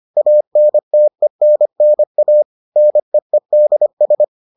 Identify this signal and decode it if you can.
Morse code, heard as a keyed tone, about 25 wpm, 600 Hz. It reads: ANTENNA NEEDS